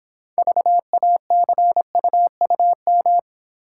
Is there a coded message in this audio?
Yes: Morse code from 0.4 to 3.2 s, heard as a keyed tone, about 26 wpm, 700 Hz. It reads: VACUUM